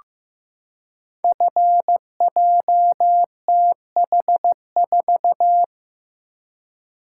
Morse code, heard as a keyed tone, about 15 wpm, 700 Hz. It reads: FJTH4
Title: Code Groups - 5 Letter:Numbers - 15wpm